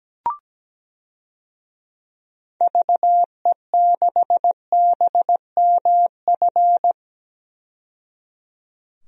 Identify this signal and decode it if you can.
Morse code, heard as a keyed tone, about 17 wpm, 700 Hz. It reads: VE6BMF
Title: DXCC-100 Call Signs - 6 Characters - 17wpm